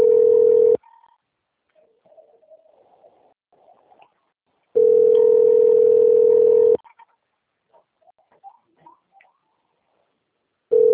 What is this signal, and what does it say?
Ringback tone (North America), 440 + 480 Hz.